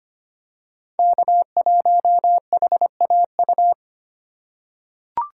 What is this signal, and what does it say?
Morse code, heard as a keyed tone, about 25 wpm, 700 Hz. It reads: K1HAU
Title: USA Call Signs - 5 Characters - 4x Character Spacing - 25wpm